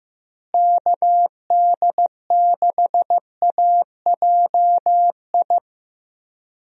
Morse code, heard as a keyed tone, about 15 wpm, 700 Hz. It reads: KD6AJI
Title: USA Call Signs - 6 Characters - Repeated 2x - 15wpm